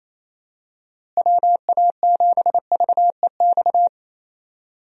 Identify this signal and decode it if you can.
Morse code, heard as a keyed tone, about 28 wpm, 700 Hz. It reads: WA7VEX
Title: USA Call Signs - 6 Characters - Repeated 2x - 28wpm